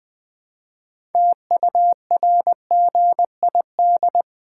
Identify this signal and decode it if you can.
Morse code, heard as a keyed tone, about 20 wpm, 700 Hz. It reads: TURGID
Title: Pangram Challenge v2 - 20wpm